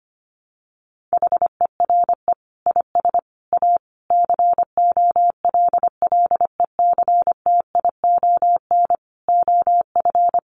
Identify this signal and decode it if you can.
Morse code, heard as a keyed tone, about 25 wpm, 700 Hz. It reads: HERE IS A COLLECTION OF
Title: Sentences from Top 1200 Words 25wpm